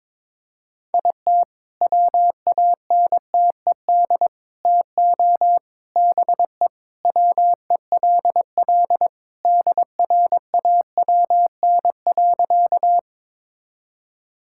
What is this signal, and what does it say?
Morse code, heard as a keyed tone, about 22 wpm, 690 Hz. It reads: IT WANTED TO BE WELL DRAWN.
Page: morse-code-ninja-practice-22wpm